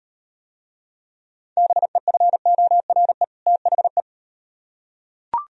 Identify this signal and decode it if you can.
Morse code, heard as a keyed tone, about 38 wpm, 700 Hz. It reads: BEFORE THE